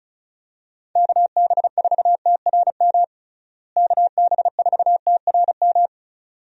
Morse code, heard as a keyed tone, about 36 wpm, 700 Hz. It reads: KB4TRM KB4TRM
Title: USA Call Signs - 6 Characters - Repeated 2x - 35wpm